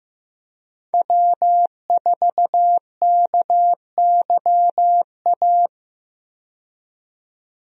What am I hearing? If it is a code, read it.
Morse code, heard as a keyed tone, about 15 wpm, 700 Hz. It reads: W4KYA
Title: USA Call Signs - 5 Characters - 15wpm